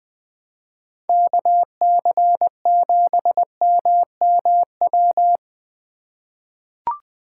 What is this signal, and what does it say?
Morse code, heard as a keyed tone, about 20 wpm, 700 Hz. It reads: KC7MMW
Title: USA Call Signs - 6 Characters - 3x Character Spacing - 20wpm